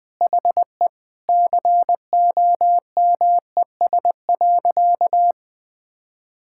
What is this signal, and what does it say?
Morse code, heard as a keyed tone, about 20 wpm, 700 Hz. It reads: HE COMES.
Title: Sentences with 2-Word Phrases - 20wpm